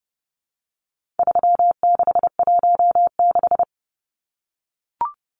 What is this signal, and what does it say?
Morse code, heard as a keyed tone, about 30 wpm, 700 Hz. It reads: 3616
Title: Numbers - 4-Digit 30wpm